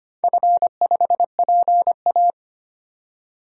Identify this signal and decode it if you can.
Morse code, heard as a keyed tone, about 25 wpm, 700 Hz. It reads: F5PA
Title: DXCC-100 Call Signs - 4 Characters - 25wpm